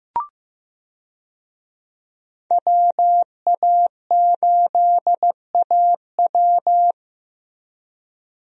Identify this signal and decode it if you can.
Morse code, heard as a keyed tone, about 15 wpm, 700 Hz. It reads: WA8AW
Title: USA Call Signs - 5 Characters - Repeated 2x - 15wpm